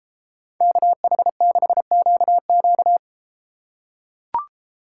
Morse code, heard as a keyed tone, about 31 wpm, 700 Hz. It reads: KH6QQ